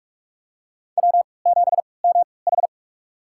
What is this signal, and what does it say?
Morse code, heard as a keyed tone, about 46 wpm, spaced as for 25 wpm, 700 Hz. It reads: W7MH
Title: USA Call Signs - 4 Characters - 3x Character Spacing - 45wpm